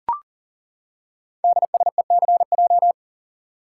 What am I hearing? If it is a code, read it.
Morse code, heard as a keyed tone, about 39 wpm, 700 Hz. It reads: DSECJ
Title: Code Groups - 5 Letters - 40wpm